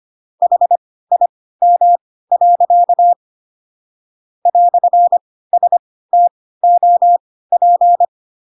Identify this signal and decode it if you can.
Morse code, heard as a keyed tone, about 25 wpm, spaced as for 16 wpm, 700 Hz. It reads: HIM. "STOP
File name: Pinocchio / Pinocchio ch3 10 25 ik1hge